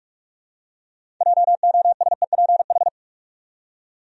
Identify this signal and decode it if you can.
Morse code, heard as a keyed tone, about 46 wpm, 700 Hz. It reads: JOSEPH
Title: USA Common Names - 45wpm